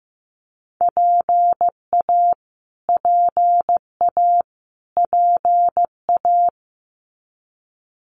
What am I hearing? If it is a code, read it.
Morse code, heard as a keyed tone, about 15 wpm, 700 Hz. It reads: PA PA PA
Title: US States Repeated 3x 15wpm